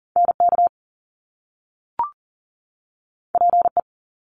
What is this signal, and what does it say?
Morse code, heard as a keyed tone, about 38 wpm, 700 Hz. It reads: NK PE
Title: Top Two Letter Combinations in Words 40wpm